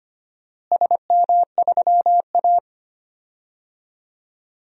Morse code, heard as a keyed tone, about 25 wpm, 700 Hz. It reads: SM3A
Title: DXCC-100 Call Signs - 4 Characters - Repeated 2x - 25wpm